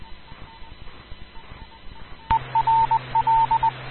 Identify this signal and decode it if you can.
Morse code, heard as a keyed tone, about 20 wpm, 900 Hz, with radio noise. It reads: ERL
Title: Rocky Mountain Radio League Repeaters